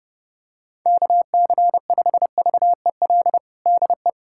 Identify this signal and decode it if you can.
Morse code, heard as a keyed tone, about 32 wpm, 700 Hz. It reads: KC5VEL DE
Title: USA Callsign DE Callsign - 6 Characters - 30wpm